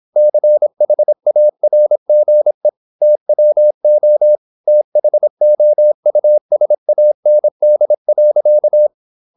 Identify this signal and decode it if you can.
Morse code, heard as a keyed tone, about 26 wpm, 600 Hz. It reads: CHARGE TWO THOUSAND.